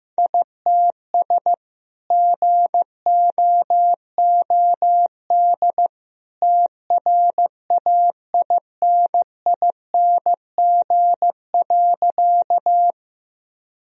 Morse code, heard as a keyed tone, about 15 wpm, 700 Hz. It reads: ITS GOOD TRAINING.
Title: Sentences from Top 1000 Words - Review Part 4 - 15wpm